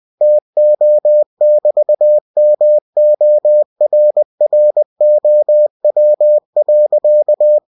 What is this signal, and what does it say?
Morse code, heard as a keyed tone, about 20 wpm, 600 Hz. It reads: TO<BT>MORROW.
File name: cw-books / winnie-the-pooh / winnie-20wpm-page-0024